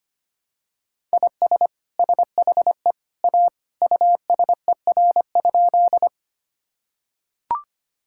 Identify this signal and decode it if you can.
Morse code, heard as a keyed tone, about 25 wpm, 700 Hz. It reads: IS SHE A USER?
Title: Sentences from Top 1000 Words - Review Part 5 - 25wpm